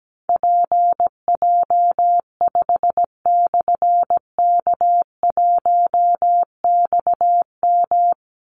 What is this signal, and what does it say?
Morse code, heard as a keyed tone, about 17 wpm, 700 Hz. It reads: PJ5/K1XM
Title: Call Signs Repeated 3x - 17wpm